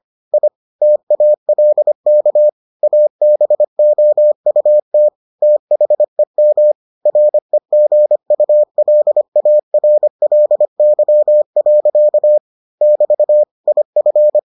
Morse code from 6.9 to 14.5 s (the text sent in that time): REGULARLY. <BT> IF